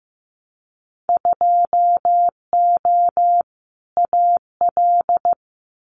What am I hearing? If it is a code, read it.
Morse code, heard as a keyed tone, about 15 wpm, 700 Hz. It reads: 2O AL
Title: Winter Field Day Exchanges Encore 15wpm